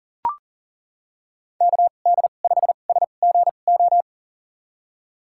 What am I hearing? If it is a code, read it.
Morse code, heard as a keyed tone, about 40 wpm, spaced as for 29 wpm, 700 Hz. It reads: KD5SGO
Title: USA Call Signs - 6 Characters - 2x Character Spacing - 40wpm